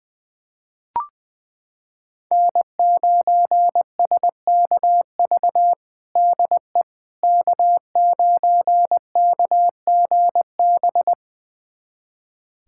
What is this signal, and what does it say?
Morse code, heard as a keyed tone, about 20 wpm, 700 Hz. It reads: N9SKV DE K9KGB
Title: USA Callsign DE Callsign - 5 Characters - 20wpm